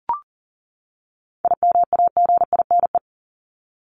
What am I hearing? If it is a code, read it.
Morse code, heard as a keyed tone, about 40 wpm, 700 Hz. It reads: IMAGINE
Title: Top 1251-1500 Words 40wpm